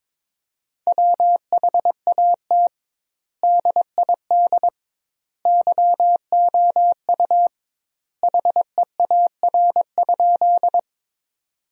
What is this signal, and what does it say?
Morse code, heard as a keyed tone, about 22 wpm, 700 Hz. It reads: WHAT DID YOU HEAR?
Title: 2x Word-Spacing - Sentences from Top 300 Words - 22wpm